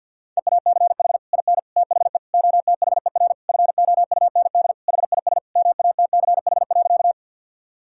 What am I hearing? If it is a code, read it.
Morse code, heard as a keyed tone, about 50 wpm, 700 Hz. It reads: EACH IN THE OTHER FOUND HIS MATCH.
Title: Sentences from Top 1600 Words 50wpm